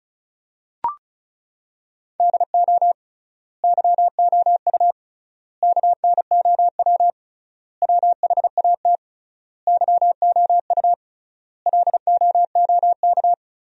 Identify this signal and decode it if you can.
Morse code, heard as a keyed tone, about 34 wpm, 700 Hz. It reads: DO YOU KNOW WHAT YOU LOOK
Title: 3x Word-Spacing - Sentences from Top 100 Words - 35wpm